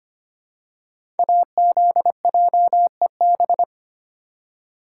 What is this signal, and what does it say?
Morse code, heard as a keyed tone, about 25 wpm, 700 Hz. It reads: AZJEB
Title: Code Groups - 5 Letter:Numbers - 25wpm